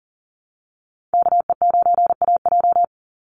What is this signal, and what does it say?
Morse code, heard as a keyed tone, about 40 wpm, 700 Hz. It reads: KE9AJ
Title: NA SOTA Chaser Callsigns 40wpm